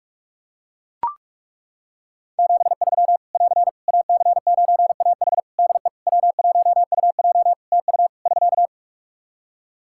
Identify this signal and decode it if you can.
Morse code, heard as a keyed tone, about 42 wpm, 700 Hz. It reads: 73 <AR> AC9AH DE W1UJ TU <SK>
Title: K1USN SST QSO Practice 45wpm